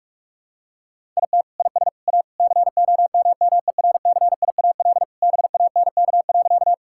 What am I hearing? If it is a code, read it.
Morse code, heard as a keyed tone, about 44 wpm, 700 Hz. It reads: IT IS A COMMERCIAL BANK.
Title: Sentences from Top 1000 Words - Review Part 6 - 45wpm